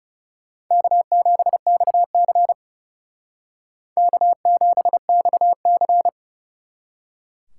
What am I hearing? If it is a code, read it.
Morse code, heard as a keyed tone, about 37 wpm, 700 Hz. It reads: K7XC K7XC
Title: Speed Racing - USA 1x2 Call Signs 30wpm